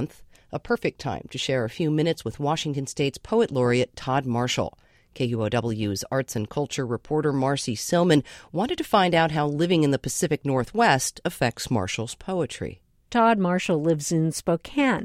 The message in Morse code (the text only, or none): none